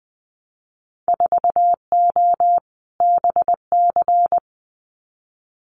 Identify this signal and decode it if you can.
Morse code, heard as a keyed tone, about 19 wpm, 700 Hz. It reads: 4O BC
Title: Winter Field Day Exchanges Encore 20wpm